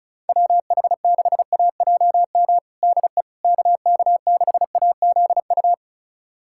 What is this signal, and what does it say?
Morse code, heard as a keyed tone, about 34 wpm, 700 Hz. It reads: WH6AJM DE KK6AZU
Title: USA Callsign DE Callsign 35wpm